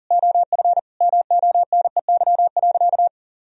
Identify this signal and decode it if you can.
Morse code, heard as a keyed tone, about 40 wpm, 700 Hz. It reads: OF MONEY.
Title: Sentences from Top 1000 Words - Review Part 3 - 40wpm